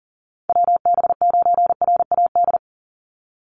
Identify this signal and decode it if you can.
Morse code, heard as a keyed tone, about 40 wpm, 700 Hz. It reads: WB9RAD
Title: USA Call Signs 40wpm